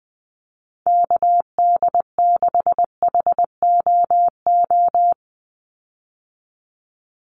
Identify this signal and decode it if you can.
Morse code, heard as a keyed tone, about 20 wpm, 700 Hz. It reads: KD6HOO
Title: USA Call Signs 20wpm